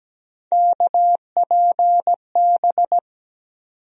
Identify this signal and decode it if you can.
Morse code, heard as a keyed tone, about 17 wpm, 700 Hz. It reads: KPB
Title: USA Call Sign Postfixes - 3 Letters 17wpm